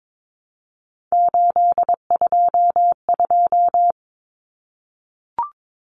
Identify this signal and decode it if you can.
Morse code, heard as a keyed tone, about 22 wpm, 700 Hz. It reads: 822
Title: Numbers - 3-Digit 22wpm